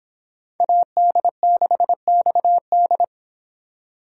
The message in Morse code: AD6XD

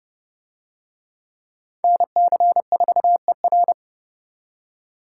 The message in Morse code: NC4ER

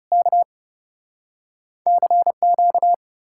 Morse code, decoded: K CQ

